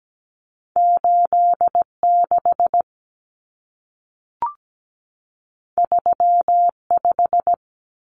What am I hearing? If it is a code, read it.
Morse code, heard as a keyed tone, about 17 wpm, 700 Hz. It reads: 8 6 3 5
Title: Common Cut Numbers - 2-Digit 17wpm